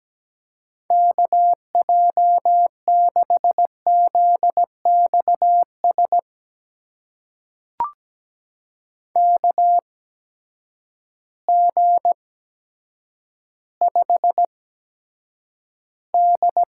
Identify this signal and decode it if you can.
Morse code, heard as a keyed tone, about 17 wpm, spaced as for 8 wpm, 700 Hz. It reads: KJ6ZXS KG5D
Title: USA Call Signs - 6 Characters - 8x Character Spacing - 17wpm